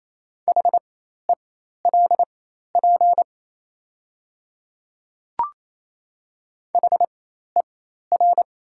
Morse code, heard as a keyed tone, about 28 wpm, spaced as for 13 wpm, 700 Hz. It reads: HELP HER
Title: 4x Character Spacing - Top 300 Words - 28wpm